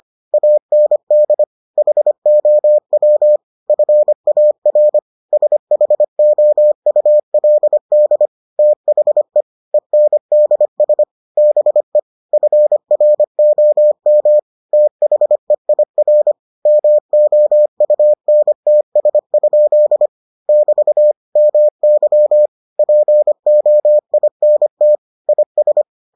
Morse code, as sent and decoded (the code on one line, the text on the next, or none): .- -. -.. / .... --- .-- / ..-. .- .-. / ... .... --- ..- .-.. -.. / - .... / . -. -.. ... / -... / ..-. .-. --- -- / - .... .. .-. / -- --- ..- -. - ... ..--.. / -...- / -- -.-- / .--. --- .. -. - / .. ...
AND HOW FAR SHOULD THE ENDS BE FROM THEIR MOUNTS? <BT> MY POINT IS